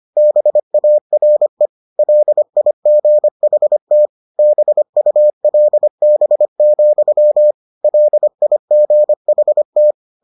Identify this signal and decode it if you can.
Morse code, heard as a keyed tone, about 25 wpm, 600 Hz. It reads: BARE LIGHT BULB, LIGHT